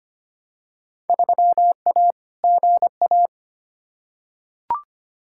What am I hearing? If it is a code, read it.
Morse code, heard as a keyed tone, about 25 wpm, 700 Hz. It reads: 3A GA
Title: Field Day Exchanges - 25wpm